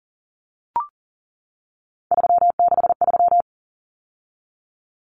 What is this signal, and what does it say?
Morse code, heard as a keyed tone, about 40 wpm, 700 Hz. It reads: 363
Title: Common Cut Numbers - 3-Digit 40wpm